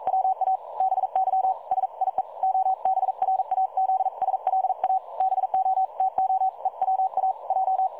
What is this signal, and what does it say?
Morse code, heard as a keyed tone, about 42 wpm, 760 Hz. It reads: WABY SIE OBRAZSLA BO TO EWA J
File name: dlazaawansowanych / nagraniazpasma / websdr recording 3565.0kHz